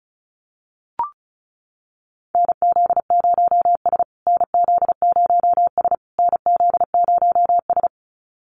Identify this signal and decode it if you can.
Morse code, heard as a keyed tone, about 34 wpm, 700 Hz. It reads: NZ0S NZ0S NZ0S
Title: USA Call Signs Repeated 3x 35wpm